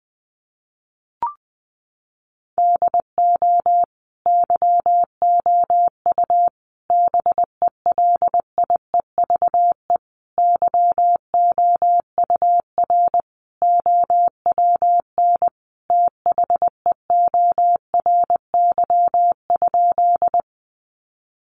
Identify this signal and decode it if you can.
Morse code, heard as a keyed tone, about 20 wpm, 700 Hz. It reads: DO YOU BELIEVE YOUR OWN THEORY?
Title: Sentences from Top 1000 Words 20wpm